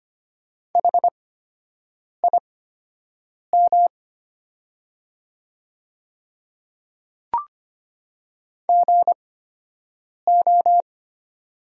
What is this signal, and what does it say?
Morse code, heard as a keyed tone, about 25 wpm, spaced as for 7 wpm, 700 Hz. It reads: HIM GO
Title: 8x Character Spacing - Top 100 Words - 25wpm